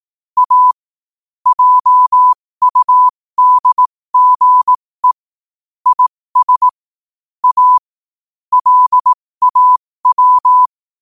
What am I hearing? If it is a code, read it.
Morse code, heard as a keyed tone, about 18 wpm, spaced as for 15 wpm, 1000 Hz. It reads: A JUDGE IS A LAW
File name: 15wpm